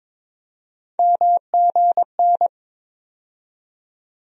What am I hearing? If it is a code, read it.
Morse code, heard as a keyed tone, about 22 wpm, 700 Hz. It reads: MGN
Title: USA Call Sign Postfixes - 3 Letters 22wpm